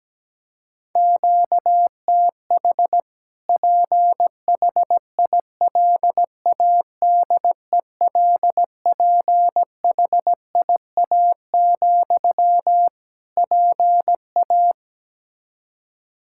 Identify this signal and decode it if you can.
Morse code, heard as a keyed tone, about 17 wpm, 700 Hz. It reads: QTH PHILADELPHIA, PA
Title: Practice QSO - Exchange 1 - Beginning - 17wpm